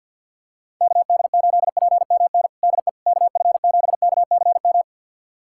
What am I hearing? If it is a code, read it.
Morse code, heard as a keyed tone, about 53 wpm, 700 Hz. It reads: KD8PMN DE KF7KCM
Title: USA Callsign DE Callsign - 6 Characters - 50wpm